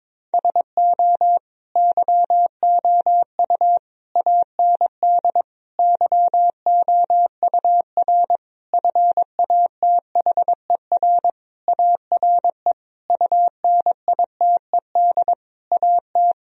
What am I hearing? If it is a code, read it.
Morse code, heard as a keyed tone, about 22 wpm, 700 Hz. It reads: SO YOU AND YOUR FATHER ARE UNITED AT
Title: Sentences from Top 1000 Words - Review Part 4 - 22wpm